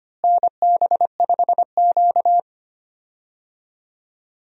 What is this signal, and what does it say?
Morse code, heard as a keyed tone, about 25 wpm, 700 Hz. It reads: NB5Q
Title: USA Call Signs - 4 Characters - Repeated 3x - 25wpm